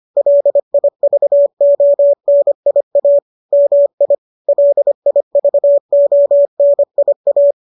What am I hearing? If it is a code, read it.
Morse code, heard as a keyed tone, about 25 wpm, 570 Hz. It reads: LIVONIA MI LIVONIA